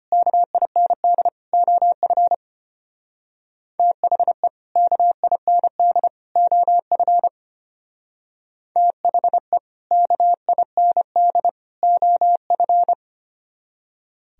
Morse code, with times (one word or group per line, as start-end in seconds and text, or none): 0.1-1.3 KIND
1.5-2.3 OF
3.8-4.5 THE
4.8-6.1 KIND
6.4-7.3 OF
8.8-9.6 THE
9.9-11.5 KIND
11.8-12.9 OF